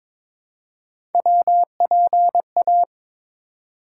Morse code, heard as a keyed tone, about 22 wpm, 700 Hz. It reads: WPA
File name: ARRL Sections v2 - 22wpm